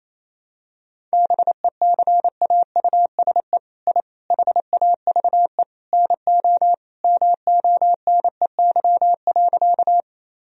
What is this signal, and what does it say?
Morse code, heard as a keyed tone, about 28 wpm, 700 Hz. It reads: BECAUSE I HAVE NO MONEY.
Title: Sentences from Top 1000 Words - Review Part 3 - 28wpm